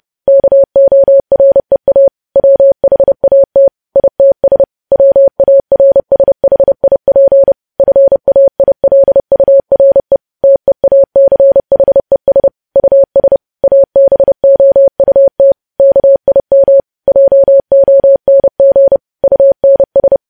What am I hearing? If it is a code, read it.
Morse code, heard as a keyed tone, about 30 wpm, 550 Hz. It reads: KOREA WHAT ITS WARSHIP FAILURE TEACHES US ABOUT KIM JONG UNS
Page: news headlines in morse code at 30 wpm